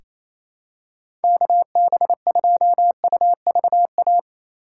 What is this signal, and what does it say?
Morse code, heard as a keyed tone, about 28 wpm, 700 Hz. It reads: KB2UVA
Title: USA Call Signs - 6 Characters - 2x Character Spacing - 28wpm